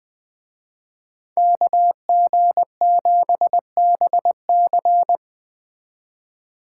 Morse code, heard as a keyed tone, about 21 wpm, 700 Hz. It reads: KG7BC
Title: Speed Racing - USA 2x2 Call Signs 17wpm